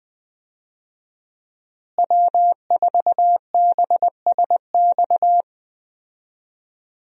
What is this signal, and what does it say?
Morse code, heard as a keyed tone, about 20 wpm, 700 Hz. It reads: W4BSX
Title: USA Call Signs - 5 Characters - Repeated 2x - 20wpm